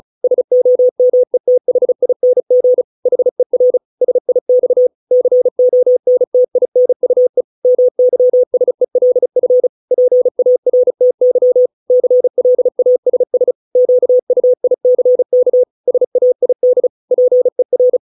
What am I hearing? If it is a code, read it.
Morse code, heard as a keyed tone, about 36 wpm, 500 Hz. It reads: SOMETHING HER SIX CONTINUE MYSELF PARTY CLASS QUICK SAID PER